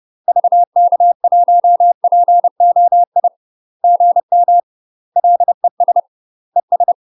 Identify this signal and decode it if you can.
Morse code, heard as a keyed tone, about 30 wpm, 700 Hz. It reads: VK1POI GM LES ES
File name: UK MASTER QSOs TRAINING / 30WPM / UK MASTER QSOs 0046